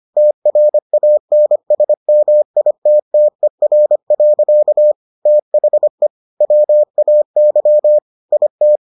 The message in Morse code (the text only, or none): TRANSMITTER. THE WAY IT